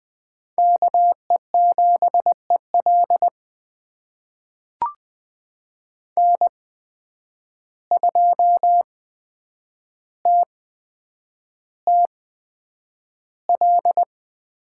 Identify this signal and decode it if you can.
Morse code, heard as a keyed tone, about 20 wpm, spaced as for 9 wpm, 700 Hz. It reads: KE7EL N2TTL
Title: USA Call Signs - 5 Characters - 8x Character Spacing - 20wpm